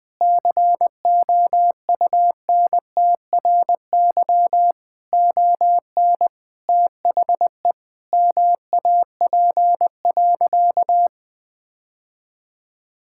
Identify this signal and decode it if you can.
Morse code, heard as a keyed tone, about 20 wpm, 700 Hz. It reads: COUNTRY ON THE MAP.
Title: Sentences from Top 1000 Words - Review Part 6 - 20wpm